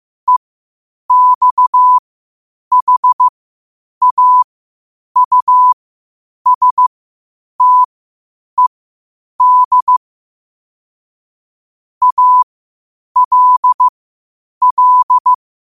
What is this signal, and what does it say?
Morse code, heard as a keyed tone, about 15 wpm, spaced as for 8 wpm, 1000 Hz. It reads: EXHAUSTED ALL
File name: qotdfiles / 8wpm